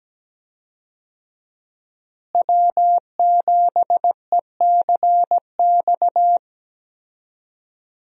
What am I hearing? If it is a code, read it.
Morse code, heard as a keyed tone, about 17 wpm, 700 Hz. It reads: W7ECX
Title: USA Call Signs - 5 Characters - Repeated 2x - 17wpm